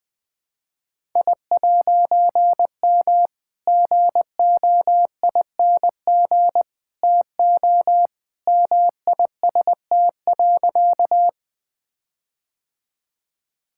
Morse code, heard as a keyed tone, about 20 wpm, 700 Hz. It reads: I'M GOING TO MIST.
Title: All-Age Appropriate Jokes - 20wpm